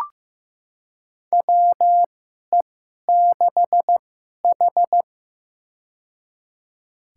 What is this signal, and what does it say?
Morse code, heard as a keyed tone, about 15 wpm, spaced as for 11 wpm, 700 Hz. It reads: WE6H
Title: USA Call Signs - 4 Characters - 2x Character Spacing - 15wpm